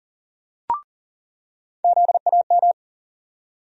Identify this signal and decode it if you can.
Morse code, heard as a keyed tone, about 40 wpm, 700 Hz. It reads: ZAM
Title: USA Call Sign Postfixes - 3 Letters 40wpm